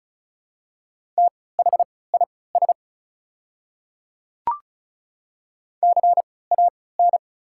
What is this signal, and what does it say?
Morse code, heard as a keyed tone, about 36 wpm, spaced as for 20 wpm, 700 Hz. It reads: THIS CAN
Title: 3x Character Spacing - Top 200 Words - 35wpm